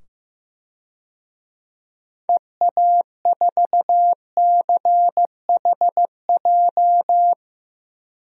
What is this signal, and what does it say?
Morse code, heard as a keyed tone, about 15 wpm, 700 Hz. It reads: EA4CHJ
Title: DXCC-100 Call Signs - 6 Characters - 15wpm